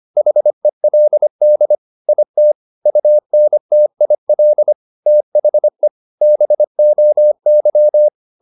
Morse code, heard as a keyed tone, about 25 wpm, 600 Hz. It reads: HELD IT UNTIL THE BOY
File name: the old man and the sea part 1 25wpm